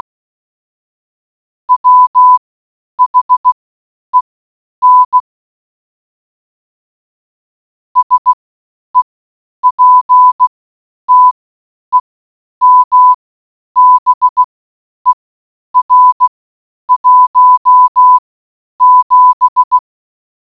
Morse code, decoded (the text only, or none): WHEN SEPTEMBER 17